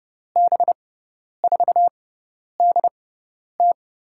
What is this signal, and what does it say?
Morse code, heard as a keyed tone, about 30 wpm, spaced as for 10 wpm, 700 Hz. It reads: B4DT
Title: USA Call Signs - 5 Characters - 6x Character Spacing - 30wpm